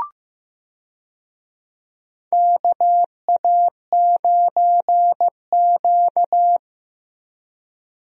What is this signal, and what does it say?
Morse code, heard as a keyed tone, about 15 wpm, 700 Hz. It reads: KA9Q
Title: USA Call Signs - 4 Characters - Repeated 2x - 15wpm